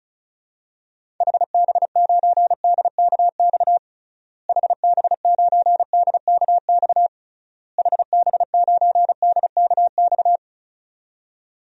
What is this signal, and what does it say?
Morse code, heard as a keyed tone, about 36 wpm, 700 Hz. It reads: HB9DKX HB9DKX HB9DKX
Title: DXCC-100 Call Signs - 6 Characters - Repeated 3x - 35wpm